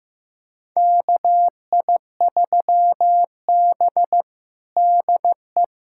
Morse code, heard as a keyed tone, about 15 wpm, 700 Hz. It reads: KI3B DE
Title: USA Callsign DE Callsign 15wpm